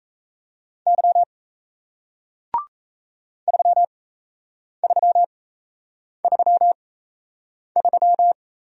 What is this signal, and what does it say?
Morse code, heard as a keyed tone, about 37 wpm, spaced as for 8 wpm, 700 Hz. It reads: Y 3333